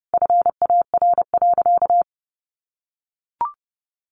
Morse code, heard as a keyed tone, about 30 wpm, 700 Hz. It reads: FAR.
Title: Sentences from Top 1500 Words 30wpm